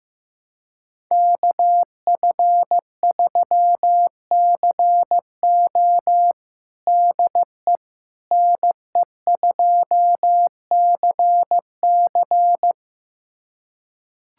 Morse code, heard as a keyed tone, about 15 wpm, 700 Hz. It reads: KF3CO DE NE2CC
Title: USA Callsign DE Callsign 15wpm